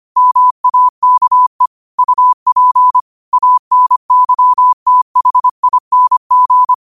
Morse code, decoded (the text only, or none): MAKE UP ANYTHING